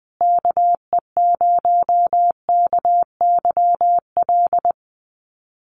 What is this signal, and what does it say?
Morse code, heard as a keyed tone, about 20 wpm, 700 Hz. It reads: KE0KYL
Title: USA Call Signs Repeated 3x 20wpm